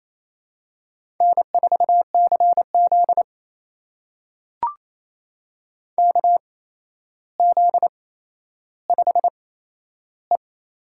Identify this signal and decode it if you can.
Morse code, heard as a keyed tone, about 28 wpm, spaced as for 12 wpm, 700 Hz. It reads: N4CZ KZ5E
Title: USA Call Signs - 8x Character Spacing - 28wpm